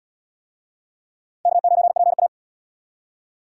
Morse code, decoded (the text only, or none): K9PA